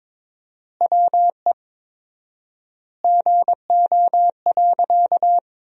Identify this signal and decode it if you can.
Morse code, heard as a keyed tone, about 22 wpm, 700 Hz. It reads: WE GO.
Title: 4x Word-Spacing - Sentences from Top 200 Words - 22wpm